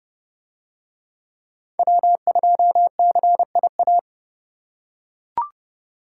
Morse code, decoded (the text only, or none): W2CIA